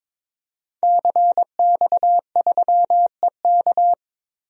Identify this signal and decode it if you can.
Morse code, heard as a keyed tone, about 22 wpm, 700 Hz. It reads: CX3EK
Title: DXCC-100 Call Signs - 5 Characters - Repeated 2x - 22wpm